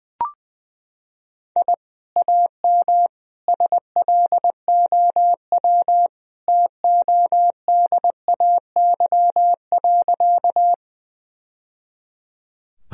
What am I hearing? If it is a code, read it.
Morse code, heard as a keyed tone, about 20 wpm, 700 Hz. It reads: I AM SLOW TODAY.
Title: Sentences from Top 1000 Words - Review Part 2 - 20wpm